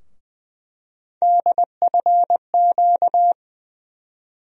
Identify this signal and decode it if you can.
Morse code, heard as a keyed tone, about 20 wpm, 700 Hz. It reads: DFQ
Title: USA Call Sign Postfixes - 3 Letters 20wpm